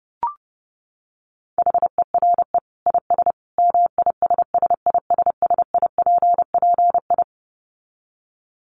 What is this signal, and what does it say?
Morse code, heard as a keyed tone, about 30 wpm, 700 Hz. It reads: HERE IS MISSISSIPPI
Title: Dit Challenge 30wpm